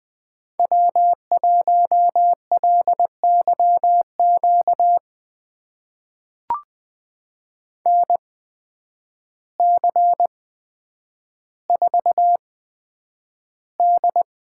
Morse code, heard as a keyed tone, about 20 wpm, spaced as for 9 wpm, 700 Hz. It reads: W1LYQ NC4D
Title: USA Call Signs - 8x Character Spacing - 20wpm